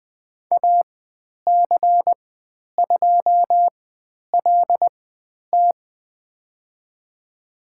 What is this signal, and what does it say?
Morse code, heard as a keyed tone, about 20 wpm, spaced as for 10 wpm, 700 Hz. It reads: AC2LT
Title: 095 Lesson 16g - Letter c - Call Signs 20wpm